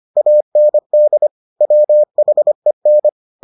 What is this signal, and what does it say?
Morse code, heard as a keyed tone, about 25 wpm, 600 Hz. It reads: AND WHEN